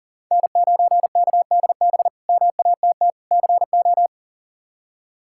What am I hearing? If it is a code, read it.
Morse code, heard as a keyed tone, about 40 wpm, 700 Hz. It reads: N9KDB MATT CO
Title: K1USN SST QSO Practice - 40wpm